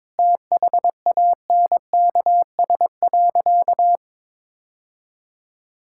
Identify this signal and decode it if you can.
Morse code, heard as a keyed tone, about 21 wpm, 700 Hz. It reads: THANKS.
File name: practice / 22wpm / Sentences with 2-Word Phrases - 22wpm